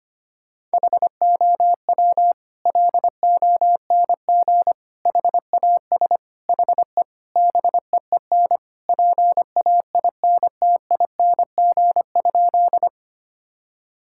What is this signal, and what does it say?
Morse code, heard as a keyed tone, about 25 wpm, 700 Hz. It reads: HOW LONG HAS HE BEEN PAINTING?
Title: Sentences from Top 1000 Words - Review Part 6 - 25wpm